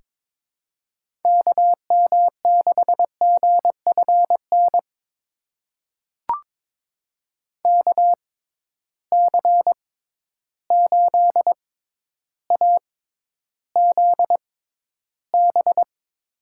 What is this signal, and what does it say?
Morse code, heard as a keyed tone, about 21 wpm, spaced as for 11 wpm, 700 Hz. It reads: KM6GFN KC8AZB
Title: USA Call Signs - 6 Characters - 6x Character Spacing - 22wpm